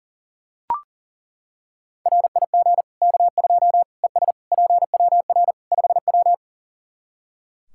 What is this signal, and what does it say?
Morse code, heard as a keyed tone, about 37 wpm, 700 Hz. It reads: RIG K2 ES PWR 5W
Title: Practice QSO - Exchange 2 - Info - 40wpm